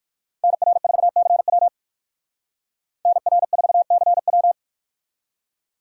Morse code, heard as a keyed tone, about 59 wpm, 700 Hz. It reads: NR4CW NR4CW